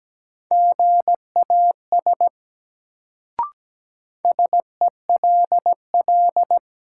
Morse code, heard as a keyed tone, about 17 wpm, 700 Hz. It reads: GAS SELL
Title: Sentences from Top 1100 Words - New Word Drill Alt - 17wpm